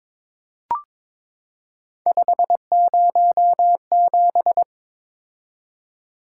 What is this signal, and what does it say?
Morse code, heard as a keyed tone, about 22 wpm, 700 Hz. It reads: 507
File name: practice / 22wpm / Three Letter-Number Combinations - 22wpm